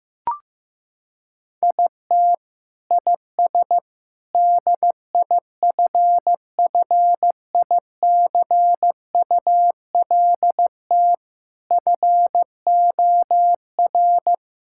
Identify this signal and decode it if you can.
Morse code, heard as a keyed tone, about 15 wpm, 700 Hz. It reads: IT IS DIFFICULT FOR
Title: Sentences from Top 1000 Words - Review Part 6 - 15wpm